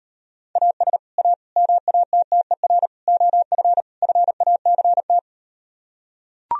Morse code, heard as a keyed tone, about 38 wpm, 700 Hz. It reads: AS A MATTER OF FACT